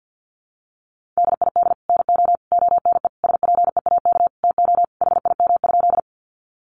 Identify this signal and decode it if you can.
Morse code, heard as a keyed tone, about 50 wpm, 700 Hz. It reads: DID NO ONE SPEAK TO HIM?
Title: Sentences from Top 1100 Words 50wpm